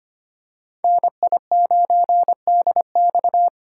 Morse code, heard as a keyed tone, about 25 wpm, 700 Hz. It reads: NI9DX
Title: USA Call Signs - 5 Characters - 2x Character Spacing - 25wpm